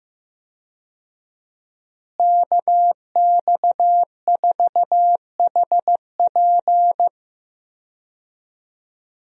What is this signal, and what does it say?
Morse code, heard as a keyed tone, about 15 wpm, 700 Hz. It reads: KX4HP
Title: USA Call Signs - 5 Characters - Repeated 3x - 15wpm